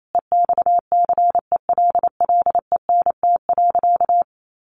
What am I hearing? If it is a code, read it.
Morse code, heard as a keyed tone, about 28 wpm, 700 Hz. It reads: EXCELLENT.
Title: Sentences from Top 1000 Words 28wpm